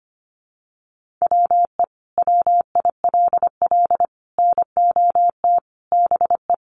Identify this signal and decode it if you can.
Morse code, heard as a keyed tone, about 25 wpm, 700 Hz. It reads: WE WILL NOT BE
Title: Sentences from Top 1300 Words 25wpm